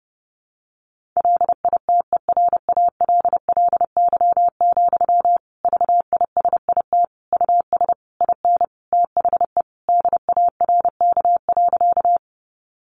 Morse code, heard as a keyed tone, about 30 wpm, 700 Hz. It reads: LITERALLY, VISIT US IN THE DARK.
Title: Sentences from Top 1400 Words 30wpm